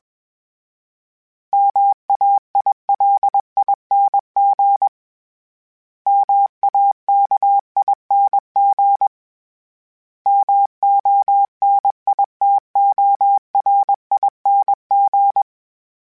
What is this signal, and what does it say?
Morse code, heard as a keyed tone, about 21 wpm, 800 Hz. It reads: MAILING MAKING MONITORING
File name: Special INGs / Special INGs-mp3s / Sngl Ing @20